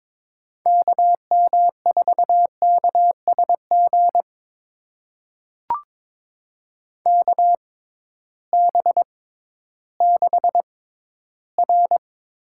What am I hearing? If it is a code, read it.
Morse code, heard as a keyed tone, about 22 wpm, spaced as for 13 wpm, 700 Hz. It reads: KM4KSG KB6R
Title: USA Call Signs - 6 Characters - 6x Character Spacing - 22wpm